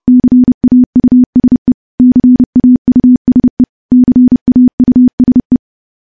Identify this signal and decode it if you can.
Morse code, heard as a keyed tone, about 30 wpm, 250 Hz. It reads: CAUSE CAUSE CAUSE
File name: morse / morse-training-30WPM